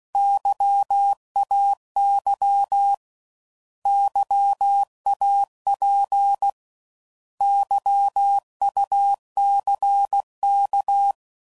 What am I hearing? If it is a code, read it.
Morse code, heard as a keyed tone, about 16 wpm, 800 Hz. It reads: YAY YAP YUCK
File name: Prefixes / By First Letter / Xs